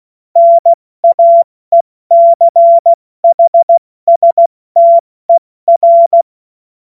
Morse code, heard as a keyed tone, about 16 wpm, 680 Hz. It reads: NAECHSTER
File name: Morsekurs / Witze / Witz 21. Dezember 2021, 70 F80